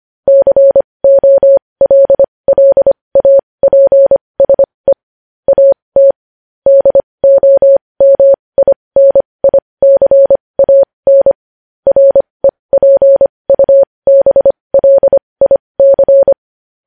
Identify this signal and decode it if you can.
Morse code, heard as a keyed tone, about 25 wpm, spaced as for 20 wpm, 550 Hz. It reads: COLLAPSE AT DOMINICAN REPUBLIC
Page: news headlines in morse code at 20 wpm